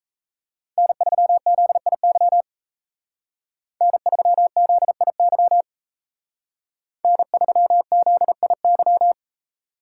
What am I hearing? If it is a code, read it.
Morse code, heard as a keyed tone, about 33 wpm, 700 Hz. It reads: N3ZIY N3ZIY N3ZIY
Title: Speed Racing - USA 1x3 Call Signs 28wpm